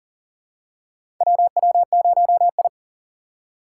Morse code, heard as a keyed tone, about 40 wpm, 690 Hz. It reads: WW0I
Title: USA Call Signs - 8x Character Spacing - 40wpm